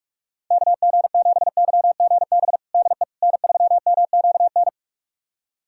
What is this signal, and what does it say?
Morse code, heard as a keyed tone, about 46 wpm, 700 Hz. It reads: KG7YGB DE N3MQN
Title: USA Callsign DE Callsign 45wpm